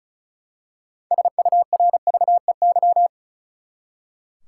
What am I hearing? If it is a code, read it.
Morse code, heard as a keyed tone, about 36 wpm, 700 Hz. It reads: SURVEY
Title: Six Letter Words - 35wpm